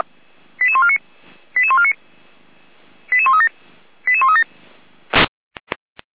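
Five-tone selective call (ZVEI), heard as quick runs of five tones, tones of about 70 ms. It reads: 88149 88149 88137 88137
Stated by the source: ZVEI five-tone selective call 88347, received via SDR.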